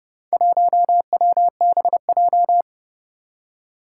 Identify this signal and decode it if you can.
Morse code, heard as a keyed tone, about 30 wpm, 700 Hz. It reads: 1WBJ
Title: Four Challenging Letter-Number Combinations 30wpm